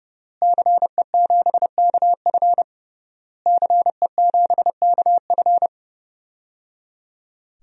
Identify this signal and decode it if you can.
Morse code, heard as a keyed tone, about 30 wpm, 700 Hz. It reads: CE7KF CE7KF
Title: DXCC-100 Call Signs - 5 Characters - Repeated 3x - 30wpm